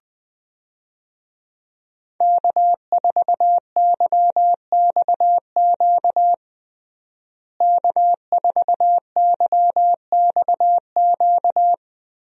Morse code, heard as a keyed tone, about 20 wpm, 700 Hz. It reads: K4YXQ K4YXQ
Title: USA Call Signs - 5 Characters - Repeated 3x - 20wpm